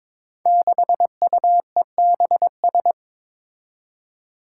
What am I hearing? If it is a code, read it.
Morse code, heard as a keyed tone, about 22 wpm, 700 Hz. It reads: 6UEBS
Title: Five Challenging Letter-Number Combinations 22wpm